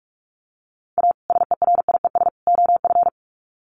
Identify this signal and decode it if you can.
Morse code, heard as a keyed tone, about 44 wpm, 700 Hz. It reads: A SERIES OF
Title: Sets of 3 Words 45wpm